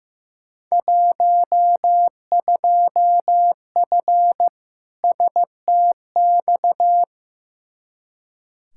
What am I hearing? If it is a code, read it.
Morse code, heard as a keyed tone, about 15 wpm, 700 Hz. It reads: 12F STX
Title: Field Day Exchanges Encore - 15wpm